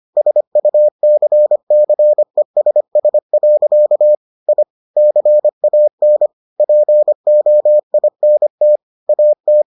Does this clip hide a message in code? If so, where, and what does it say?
Yes: Morse code from 0.2 to 9.6 s, heard as a keyed tone, about 25 wpm, 600 Hz. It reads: SUCCESS. I CAN POINT AT